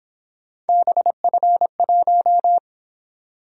Morse code, heard as a keyed tone, about 26 wpm, 700 Hz. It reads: BF1
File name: Speed-Racing - Three Letter-Number Combinations - 22wpm